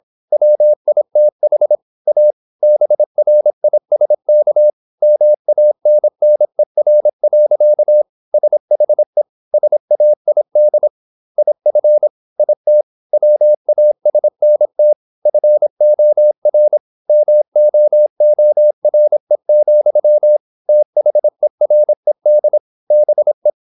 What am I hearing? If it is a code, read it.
Morse code, heard as a keyed tone, about 27 wpm, 600 Hz. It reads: WITH A BRISK MANNER. SHE SAID IF IT WASNT FOR MOORE, THERED BE
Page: Never 26wpm | CW med Gnister